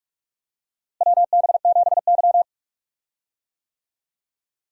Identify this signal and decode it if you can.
Morse code, heard as a keyed tone, about 46 wpm, 700 Hz. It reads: WB7Y
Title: USA Call Signs - 4 Characters - Repeated 2x - 45wpm